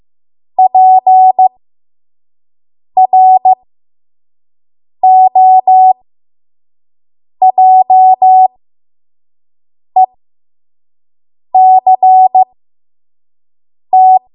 Morse code, heard as a keyed tone, about 15 wpm, spaced as for 5 wpm, 750 Hz. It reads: PROJECT
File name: Morse / 05WPM